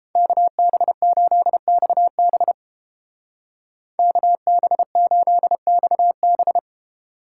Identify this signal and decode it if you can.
Morse code, heard as a keyed tone, about 31 wpm, 700 Hz. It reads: KB8XB KB8XB